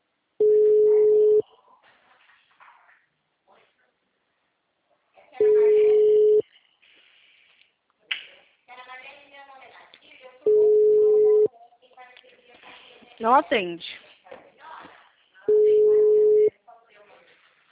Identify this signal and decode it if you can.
Ringback tone (Europe, 425 Hz), with speech.